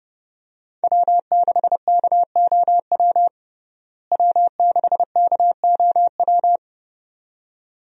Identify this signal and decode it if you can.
Morse code, heard as a keyed tone, about 30 wpm, 700 Hz. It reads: W6KOW W6KOW
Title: USA Call Signs - 5 Characters - Repeated 2x - 30wpm